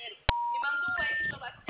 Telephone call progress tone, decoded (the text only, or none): special information tone